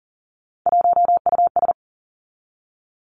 Morse code, heard as a keyed tone, about 41 wpm, 700 Hz. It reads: 1US
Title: Three Challenging Letter-Number Combinations 40wpm